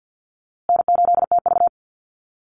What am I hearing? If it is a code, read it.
Morse code, heard as a keyed tone, about 50 wpm, 700 Hz. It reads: N8TV
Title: USA Call Signs Repeated 3x 50wpm